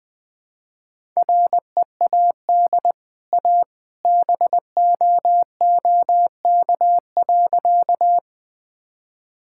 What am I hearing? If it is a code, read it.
Morse code, heard as a keyed tone, about 20 wpm, 700 Hz. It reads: READ A BOOK.
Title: Easy Sentences with Hints - 20wpm